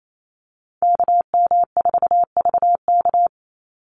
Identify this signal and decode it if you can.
Morse code, heard as a keyed tone, about 28 wpm, 700 Hz. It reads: KM4VK
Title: USA Call Signs 28wpm